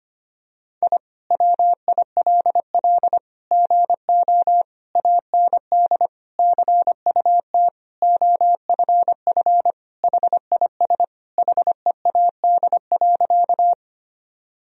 Morse code, heard as a keyed tone, about 25 wpm, 700 Hz. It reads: I WILL GO AND CUT OFF HIS HEAD.
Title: Sentences from Top 1000 Words - Review Part 1 - 25wpm